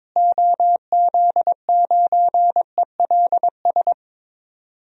Morse code, heard as a keyed tone, about 22 wpm, 700 Hz. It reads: OZ9ELS
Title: DXCC-100 Call Signs - 6 Characters - Repeated 2x - 22wpm